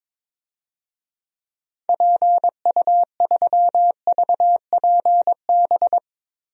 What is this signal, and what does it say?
Morse code, heard as a keyed tone, about 23 wpm, 700 Hz. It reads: PU3VPB
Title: DXCC-100 Call Signs - 6 Characters - 22wpm